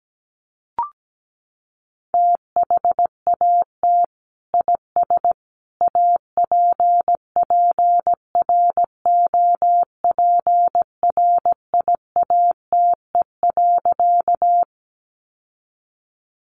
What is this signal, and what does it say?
Morse code, heard as a keyed tone, about 17 wpm, 700 Hz. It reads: THAT IS APPROPRIATE.
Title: Sentences from Top 900 Words 17wpm